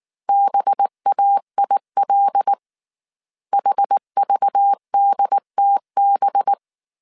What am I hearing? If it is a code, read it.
Morse code, heard as a keyed tone, about 19 wpm, 800 Hz. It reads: BAIL HVDTB